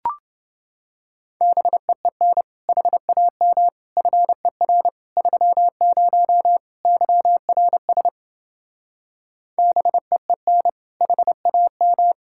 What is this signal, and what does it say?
Morse code, heard as a keyed tone, about 29 wpm, 700 Hz. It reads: BEEN HAM FER 30 YRS BEEN HAM